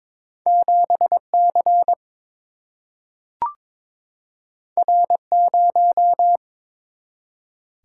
Morse code, heard as a keyed tone, about 22 wpm, 700 Hz. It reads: 7 C R 0
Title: Single-Letter Single-Number 22wpm